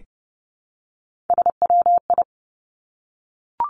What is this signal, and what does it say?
Morse code, heard as a keyed tone, about 30 wpm, 700 Hz. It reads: SWI
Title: Three Challenging Letter-Number Combinations 30wpm